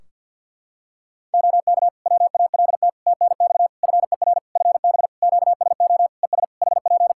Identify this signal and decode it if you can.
Morse code, heard as a keyed tone, about 49 wpm, 700 Hz. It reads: OK WALT TNX FER FB QSO ES HP
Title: Practice QSO - Exchange 4 - Ending - 50wpm